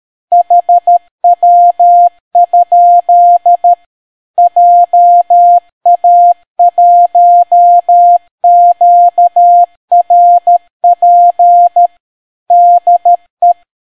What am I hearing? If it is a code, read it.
Morse code, heard as a keyed tone, about 13 wpm, 700 Hz. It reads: HW? JA1QRP DE